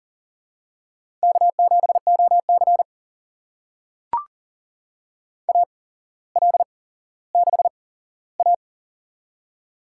Morse code, heard as a keyed tone, about 40 wpm, spaced as for 17 wpm, 700 Hz. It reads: K7OC AL6A
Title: USA Call Signs - 4 Characters - 8x Character Spacing - 40wpm